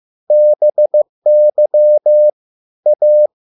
Morse code, heard as a keyed tone, about 15 wpm, 600 Hz. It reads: BY A